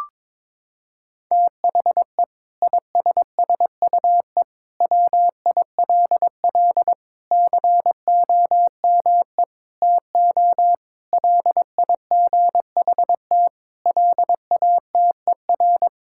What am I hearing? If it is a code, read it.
Morse code, heard as a keyed tone, about 22 wpm, 700 Hz. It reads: THE ISSUE WILL COME TO LIGHT LATER
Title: Sentences from Top 1000 Words - Review Part 6 - 22wpm